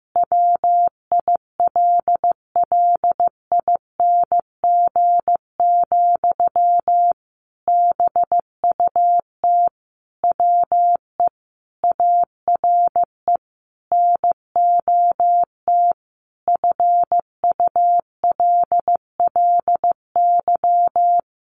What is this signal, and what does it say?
Morse code, heard as a keyed tone, about 15 wpm, 700 Hz. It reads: WILLING, BUT WE ARE NOT FULLY
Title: Sentences from Top 1300 Words 15wpm